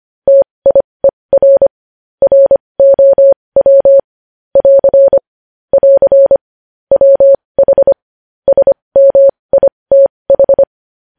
Morse code, heard as a keyed tone, about 25 wpm, spaced as for 20 wpm, 550 Hz. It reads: TIER ROW <AR> <AR> WH SMITH
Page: news-headlines-in-morse-code-at-20-wpm